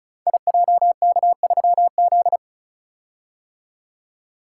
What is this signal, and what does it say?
Morse code, heard as a keyed tone, about 35 wpm, 700 Hz. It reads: IJK3Z
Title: Code Groups - 5 Letter:Numbers - 35wpm